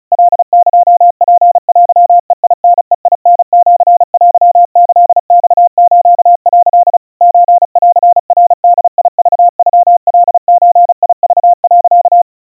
Morse code (Japanese Wabun code), heard as a keyed tone, about 35 wpm, 700 Hz. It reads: カエツテベタベタシテキマス」 ソンナボクノカゾク、
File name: kawa / Morse sample / kamyu 130 cpm 700 Hz Crispy